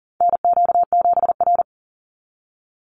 Morse code, heard as a keyed tone, about 39 wpm, 700 Hz. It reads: NQ7R
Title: NA SOTA Chaser Callsigns 40wpm